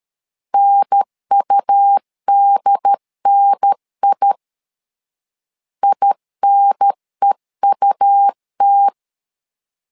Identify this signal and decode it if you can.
Morse code, heard as a keyed tone, about 13 wpm, 800 Hz. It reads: NUDNI INEUT
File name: LEZIONI / 2 UHDN / 2 UHDN 12wpm 1456